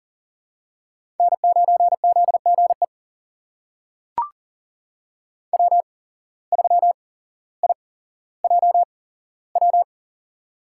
Morse code, heard as a keyed tone, about 40 wpm, spaced as for 17 wpm, 700 Hz. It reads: N9ZGE W3IJW
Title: USA Call Signs - 5 Characters - 8x Character Spacing - 40wpm